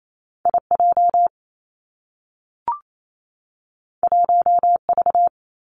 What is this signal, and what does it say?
Morse code, heard as a keyed tone, about 28 wpm, 700 Hz. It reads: I J 1 V
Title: Two Challenging Letter-Number Combinations 28wpm